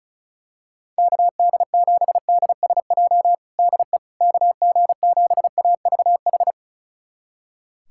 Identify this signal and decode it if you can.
Morse code, heard as a keyed tone, about 34 wpm, 700 Hz. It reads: KD7DSJ DE KG7AVH
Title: USA Callsign DE Callsign 35wpm